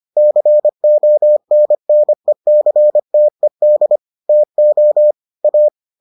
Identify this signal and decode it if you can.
Morse code, heard as a keyed tone, about 25 wpm, 600 Hz. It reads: CONNECTED TO A